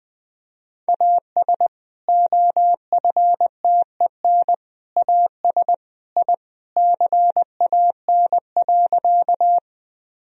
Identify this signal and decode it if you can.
Morse code, heard as a keyed tone, about 21 wpm, 700 Hz. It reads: AS OFTEN AS I CAN.
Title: Sentences from Top 1000 Words - Review Part 1 - 20wpm